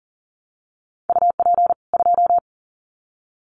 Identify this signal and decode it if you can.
Morse code, heard as a keyed tone, about 40 wpm, 700 Hz. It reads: UP 2
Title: Common QSO Phrases 40wpm